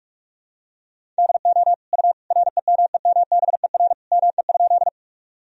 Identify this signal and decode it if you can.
Morse code, heard as a keyed tone, about 43 wpm, 700 Hz. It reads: DO U REMEMBER ME?
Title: Sentences from Top 400 Words and Abbreviations 45wpm